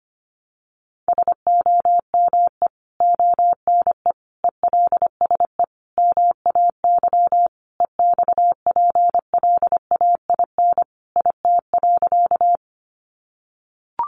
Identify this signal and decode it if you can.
Morse code, heard as a keyed tone, about 25 wpm, 700 Hz. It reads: SOME ONE ELSE MAY EXPLAIN IT.
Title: Sentences from Top 1100 Words 25wpm